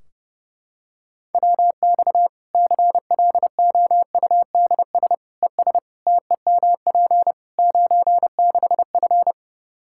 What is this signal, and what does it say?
Morse code, heard as a keyed tone, about 30 wpm, 700 Hz. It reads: WX CLOUDS ES TEMP 96F
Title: Practice QSO - Exchange 2 - Info - 30wpm